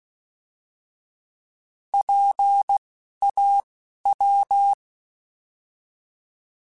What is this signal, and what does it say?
Morse code, heard as a keyed tone, about 16 wpm, 800 Hz. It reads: P A W
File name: Alphabetical / Alphabetical mp3s / Ps @10